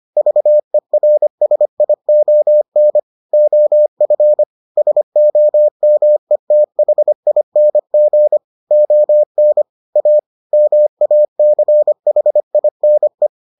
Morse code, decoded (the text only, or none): VERSION OF SOMETHING ON A MACHINE